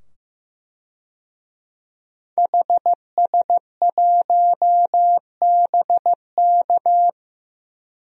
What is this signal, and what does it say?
Morse code, heard as a keyed tone, about 15 wpm, 700 Hz. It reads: HS1BK
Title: DXCC-100 Call Signs - 5 Characters - 15wpm